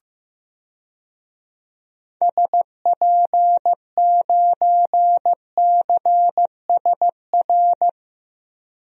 Morse code, heard as a keyed tone, about 15 wpm, 700 Hz. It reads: SP9CSR
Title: DXCC-100 Call Signs - 6 Characters - 15wpm